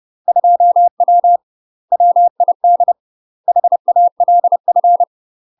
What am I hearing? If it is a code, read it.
Morse code, heard as a keyed tone, about 30 wpm, 700 Hz. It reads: 2W WID HALF